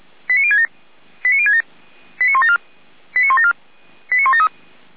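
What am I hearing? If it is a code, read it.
Five-tone selective call (ZVEI), heard as quick runs of five tones, tones of about 70 ms. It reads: 89067 89067 89174 89174 89173
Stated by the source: ZVEI five-tone selective call received via SDR.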